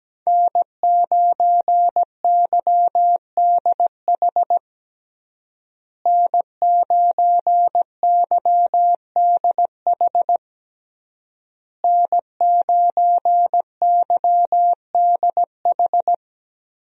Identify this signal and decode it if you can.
Morse code, heard as a keyed tone, about 17 wpm, 700 Hz. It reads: N9YDH N9YDH N9YDH
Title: USA Call Signs - 5 Characters - Repeated 3x - 17wpm